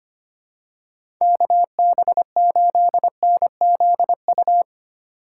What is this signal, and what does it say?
Morse code, heard as a keyed tone, about 25 wpm, 700 Hz. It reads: KB8NZU
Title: USA Call Signs - 6 Characters - 3x Character Spacing - 25wpm